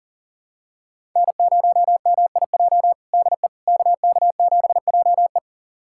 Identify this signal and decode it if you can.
Morse code, heard as a keyed tone, about 40 wpm, 700 Hz. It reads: N0MIJ DE KK7JE
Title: USA Callsign DE Callsign - 5 Characters - 40wpm